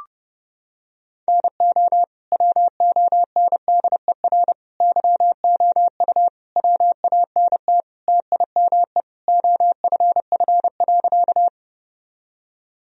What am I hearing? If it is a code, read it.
Morse code, heard as a keyed tone, about 30 wpm, 700 Hz. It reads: NO WONDER YOU WANT TIME OFF.
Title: Sentences from Top 1000 Words - Review Part 3 - 30wpm